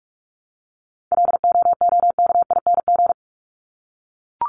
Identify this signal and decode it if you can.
Morse code, heard as a keyed tone, about 44 wpm, 700 Hz. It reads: LOOKING
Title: ING Suffix - Top 100 Words with Base Words 45wpm